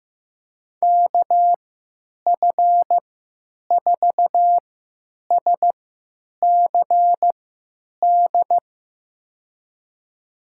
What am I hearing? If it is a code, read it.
Morse code, heard as a keyed tone, about 16 wpm, spaced as for 9 wpm, 700 Hz. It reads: KF4SCD
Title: USA Call Signs - 6 Characters - 3x Character Spacing - 15wpm